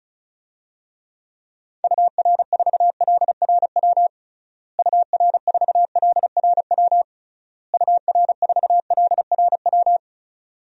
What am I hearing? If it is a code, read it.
Morse code, heard as a keyed tone, about 34 wpm, 700 Hz. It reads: UR4LRW UR4LRW UR4LRW
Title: DXCC-100 Call Signs - 6 Characters - Repeated 3x - 35wpm